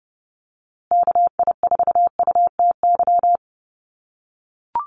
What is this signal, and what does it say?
Morse code, heard as a keyed tone, about 30 wpm, 700 Hz. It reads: KI4UTY
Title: USA Call Signs Repeated 2x 30wpm